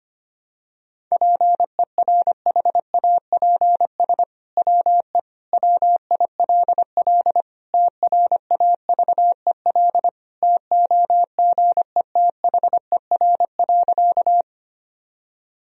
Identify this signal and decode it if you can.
Morse code, heard as a keyed tone, about 25 wpm, 700 Hz. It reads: PERHAPS WE WILL TRAVEL TOGETHER.
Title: Sentences from Top 1000 Words - Review Part 2 - 25wpm